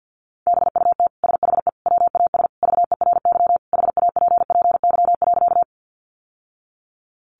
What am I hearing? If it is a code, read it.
Morse code, heard as a keyed tone, about 48 wpm, 700 Hz. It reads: BUT SHE WAS VERY HAPPY.
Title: Sentences from Top 1000 Words 50wpm